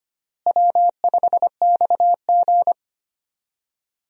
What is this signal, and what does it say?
Morse code, heard as a keyed tone, about 25 wpm, 700 Hz. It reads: W5XG
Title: USA Call Signs - 4 Characters - Repeated 2x - 25wpm